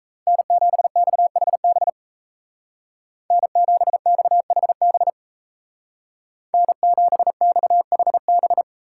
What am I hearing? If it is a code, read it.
Morse code, heard as a keyed tone, about 36 wpm, 700 Hz. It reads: N7XHB N7XHB N7XHB